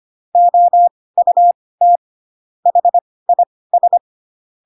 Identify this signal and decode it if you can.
Morse code, heard as a keyed tone, about 25 wpm, spaced as for 18 wpm, 700 Hz. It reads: OUT HIS